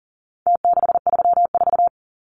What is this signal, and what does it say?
Morse code, heard as a keyed tone, about 40 wpm, 700 Hz. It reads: T634
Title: Numbers - Contest Style - 4-Digit 40wpm